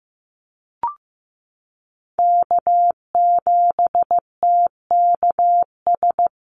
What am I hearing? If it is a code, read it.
Morse code, heard as a keyed tone, about 15 wpm, 700 Hz. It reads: K7TKS
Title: USA Call Signs Repeated 3x 15wpm